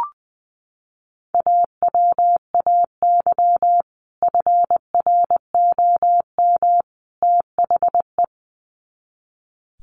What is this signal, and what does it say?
Morse code, heard as a keyed tone, about 20 wpm, 700 Hz. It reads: AWAY FROM THE
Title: Sets of 3 Words 20wpm